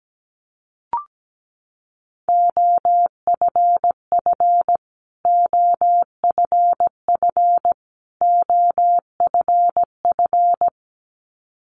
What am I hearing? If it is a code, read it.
Morse code, heard as a keyed tone, about 17 wpm, 700 Hz. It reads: OFF OFF OFF
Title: Top 200 Words Repeated 17wpm